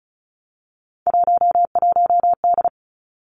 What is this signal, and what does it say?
Morse code, heard as a keyed tone, about 36 wpm, 700 Hz. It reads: 11D
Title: Three Challenging Letter-Number Combinations 35wpm